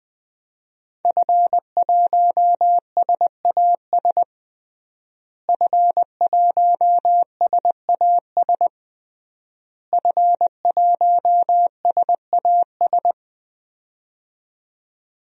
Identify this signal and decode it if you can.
Morse code, heard as a keyed tone, about 20 wpm, 700 Hz. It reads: F1SAS F1SAS F1SAS
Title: DXCC-100 Call Signs - 5 Characters - Repeated 3x - 20wpm